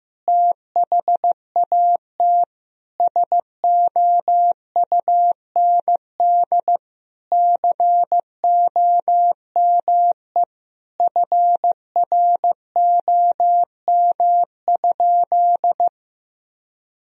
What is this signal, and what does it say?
Morse code, heard as a keyed tone, about 15 wpm, 700 Hz. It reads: THAT SOUND COME FROM?
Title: Sentences from Top 1000 Words - Review Part 1 - 15wpm